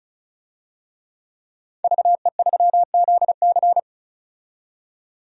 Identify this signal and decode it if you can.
Morse code, heard as a keyed tone, about 34 wpm, 700 Hz. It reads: VE3ZC
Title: DXCC-100 Call Signs - 5 Characters - 35wpm